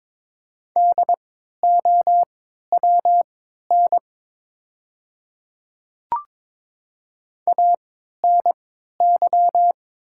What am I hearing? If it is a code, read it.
Morse code, heard as a keyed tone, about 22 wpm, spaced as for 12 wpm, 700 Hz. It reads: DOWN ANY
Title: 3x Character Spacing - Top 200 Words - 22wpm